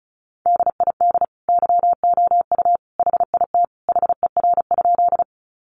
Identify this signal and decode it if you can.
Morse code, heard as a keyed tone, about 34 wpm, 700 Hz. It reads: DID YOU HIT HER?
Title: Sentences from Top 1200 Words 35wpm